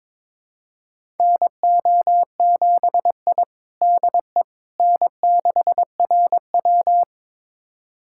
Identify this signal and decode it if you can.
Morse code, heard as a keyed tone, about 22 wpm, 700 Hz. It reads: NO7I DE N6RW
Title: USA Callsign DE Callsign - 4 Characters - 22wpm